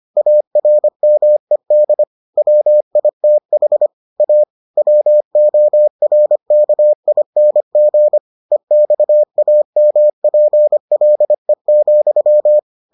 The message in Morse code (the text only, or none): ARMED WITH A WORKING EXAMPLE,